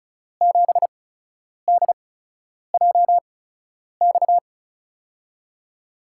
Morse code, heard as a keyed tone, about 34 wpm, spaced as for 10 wpm, 700 Hz. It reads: 7DJX
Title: USA Call Signs - 5 Characters - 8x Character Spacing - 35wpm